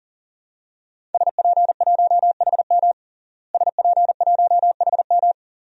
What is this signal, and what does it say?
Morse code, heard as a keyed tone, about 40 wpm, 700 Hz. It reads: SP1HM SP1HM
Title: DXCC-100 Call Signs - 5 Characters - Repeated 2x - 40wpm